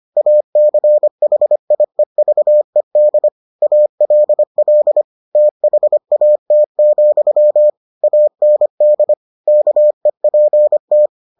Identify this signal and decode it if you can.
Morse code, heard as a keyed tone, about 25 wpm, 600 Hz. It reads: ACHIEVED ALL THAT, AND KEPT